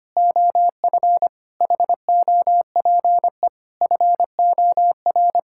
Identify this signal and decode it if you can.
Morse code, heard as a keyed tone, about 25 wpm, 700 Hz. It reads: OF HOPE FOR
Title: Sentences from Top 1000 Words - Review Part 5 - 25wpm